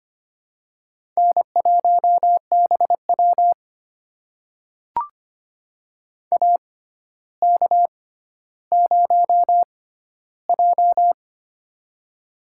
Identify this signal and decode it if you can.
Morse code, heard as a keyed tone, about 25 wpm, spaced as for 13 wpm, 700 Hz. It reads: N1BW AK0J
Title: USA Call Signs - 4 Characters - 6x Character Spacing - 25wpm